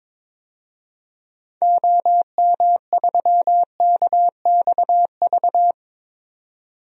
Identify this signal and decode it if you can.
Morse code, heard as a keyed tone, about 22 wpm, 700 Hz. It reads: OM3KXV
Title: DXCC-100 Call Signs - 6 Characters - 22wpm